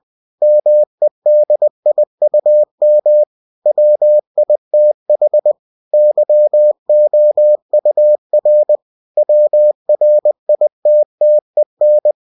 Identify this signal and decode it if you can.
Morse code, heard as a keyed tone, about 20 wpm, 600 Hz. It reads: MEDIUM WITH YOUR WRITTEN
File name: cw-books / winnie-the-pooh / winnie-20wpm-page-0119